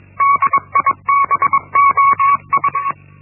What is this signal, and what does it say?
Morse code, heard as a keyed tone, about 22 wpm, 1120 Hz. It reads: DIBOH
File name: Linksmakalnis-1